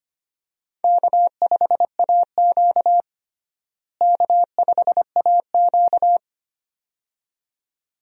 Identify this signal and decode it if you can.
Morse code, heard as a keyed tone, about 25 wpm, 700 Hz. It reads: K5AQ K5AQ
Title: USA Call Signs - 4 Characters - Repeated 2x - 25wpm